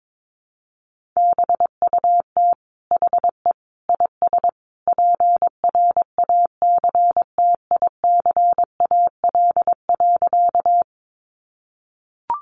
Morse code, heard as a keyed tone, about 22 wpm, 700 Hz. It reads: BUT HE IS PRACTICAL.
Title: Sentences from Top 1400 Words 22wpm